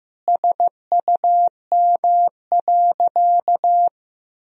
Morse code, heard as a keyed tone, about 16 wpm, 700 Hz. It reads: SUM.
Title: Sentences from Top 100 Words and Abbreviations 15wpm